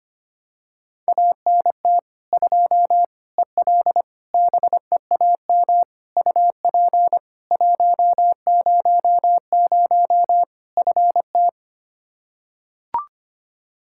Morse code, heard as a keyed tone, about 25 wpm, 700 Hz. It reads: ANT 2 EL BEAM UP 100 FT
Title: Practice QSO - Exchange 2 - Info - 25wpm